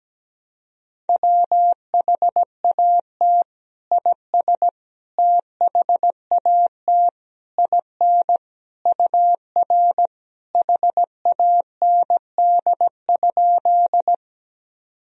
Morse code, heard as a keyed tone, about 17 wpm, 700 Hz. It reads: WHAT IS THAT IN UR HAND?
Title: Sentences from Top 200 Words and Abbreviations 17wpm